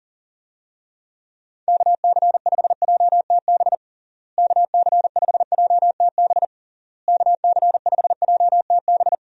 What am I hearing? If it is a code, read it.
Morse code, heard as a keyed tone, about 40 wpm, 700 Hz. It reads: KC5JTB KC5JTB KC5JTB
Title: USA Call Signs - 6 Characters - Repeated 3x - 40wpm